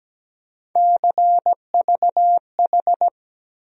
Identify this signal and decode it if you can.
Morse code, heard as a keyed tone, about 17 wpm, 700 Hz. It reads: CVH
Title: USA Call Sign Postfixes - 3 Letters 17wpm